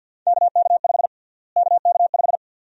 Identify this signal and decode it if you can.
Morse code, heard as a keyed tone, about 51 wpm, 700 Hz. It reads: KK5 KK5
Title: USA Call Sign Prefixes - Repeated 2x - 50wpm